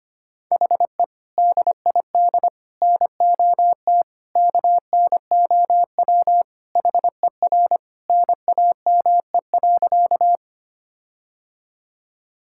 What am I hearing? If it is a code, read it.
Morse code, heard as a keyed tone, about 25 wpm, 700 Hz. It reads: HE DID NOT KNOW HER NAME.
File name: practice / 25wpm / Sentences with 4-Word Phrases - 25wpm